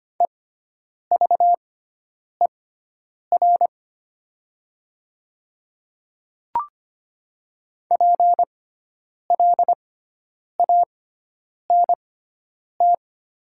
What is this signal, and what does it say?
Morse code, heard as a keyed tone, about 25 wpm, spaced as for 9 wpm, 700 Hz. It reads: EVER PLANT
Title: 6x Character Spacing - Top 500 Words - 25wpm